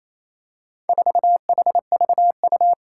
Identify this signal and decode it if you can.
Morse code, heard as a keyed tone, about 27 wpm, 700 Hz. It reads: 4HVU
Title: Four Challenging Letter-Number Combinations 28wpm